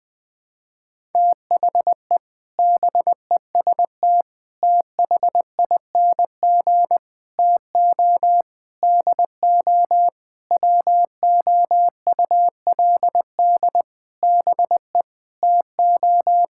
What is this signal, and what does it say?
Morse code, heard as a keyed tone, about 20 wpm, 700 Hz. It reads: THE BEST THING TO DO WOULD BE TO
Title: Sets of 8 Words - 20wpm